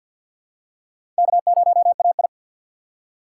Morse code, heard as a keyed tone, about 48 wpm, 700 Hz. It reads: K0AI
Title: USA Call Signs - 4x Character Spacing - 50wpm